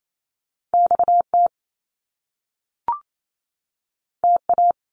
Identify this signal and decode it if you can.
Morse code, heard as a keyed tone, about 28 wpm, 700 Hz. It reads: XT TA
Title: Top Two Letter Combinations in Words 28wpm